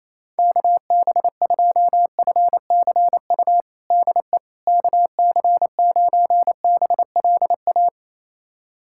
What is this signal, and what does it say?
Morse code, heard as a keyed tone, about 28 wpm, 700 Hz. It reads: KB2FCU DE KC9BLA
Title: USA Callsign DE Callsign 28wpm